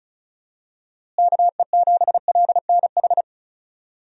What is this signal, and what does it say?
Morse code, heard as a keyed tone, about 34 wpm, 700 Hz. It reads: KE7LNH
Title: USA Call Signs - 6 Characters - Repeated 3x - 35wpm